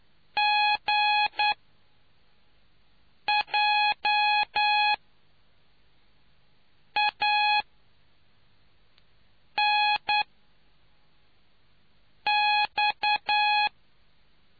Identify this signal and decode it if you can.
Morse code, heard as a keyed tone, about 10 wpm, spaced as for 4 wpm, 840 Hz. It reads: GJANX